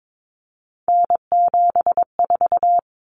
Morse code, heard as a keyed tone, about 22 wpm, 700 Hz. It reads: N74
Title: Common Cut Numbers - 3-Digit 22wpm